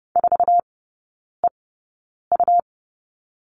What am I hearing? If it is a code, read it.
Morse code, heard as a keyed tone, about 30 wpm, spaced as for 9 wpm, 700 Hz. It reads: 4EU
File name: Lesson 18e - Letter m - Three Characters 30wpm